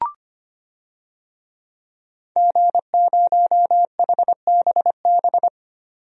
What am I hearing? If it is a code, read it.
Morse code, heard as a keyed tone, about 25 wpm, 700 Hz. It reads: G0HBB
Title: DXCC-100 Call Signs - 5 Characters - 25wpm